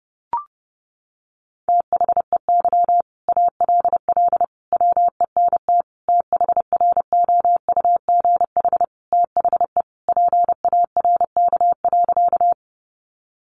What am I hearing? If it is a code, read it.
Morse code, heard as a keyed tone, about 30 wpm, 700 Hz. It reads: THEY ALL WENT THROUGH THE PARK.
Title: Sentences from Top 1300 Words 30wpm